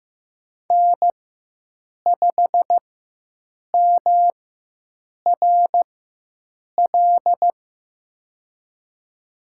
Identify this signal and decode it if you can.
Morse code, heard as a keyed tone, about 15 wpm, spaced as for 7 wpm, 700 Hz. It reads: N5MRL
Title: USA Call Signs - 5 Characters - 4x Character Spacing - 15wpm